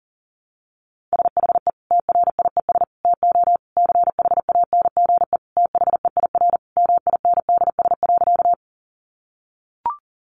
Morse code, heard as a keyed tone, about 40 wpm, 700 Hz. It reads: SHE TRIES TO CHANGE THEIR MINDS.
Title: Easy Phrases 40wpm